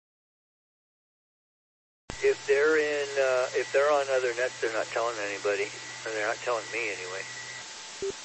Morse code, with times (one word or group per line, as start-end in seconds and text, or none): none